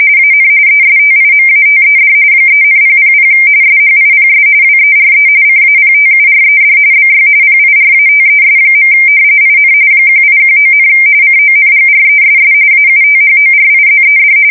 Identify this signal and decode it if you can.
RTTY, about 45.45 baud. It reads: AN ESTIMATED 3,000 SHIPS WERE CAPTURED BY PIRATES. IN 1819, CONGRESS AUTHORIZED
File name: fsk170 / usnavy2